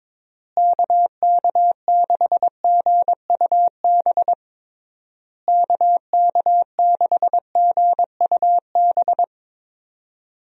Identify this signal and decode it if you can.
Morse code, heard as a keyed tone, about 22 wpm, 700 Hz. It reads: KK6GUB KK6GUB
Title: USA Call Signs - 6 Characters - Repeated 3x - 22wpm